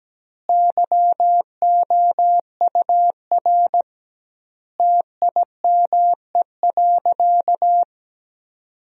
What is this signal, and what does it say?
Morse code, heard as a keyed tone, about 17 wpm, 700 Hz. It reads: YOUR TIME.
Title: 2x Word-Spacing - Sentences from Top 100 Words - 17wpm